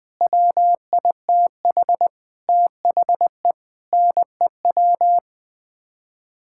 Morse code, heard as a keyed tone, about 21 wpm, 700 Hz. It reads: WITH THE NEW
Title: Sets of 9 Words - 20wpm